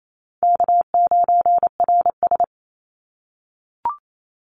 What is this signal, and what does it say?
Morse code, heard as a keyed tone, about 28 wpm, 700 Hz. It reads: K9RS
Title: Call Signs - 28wpm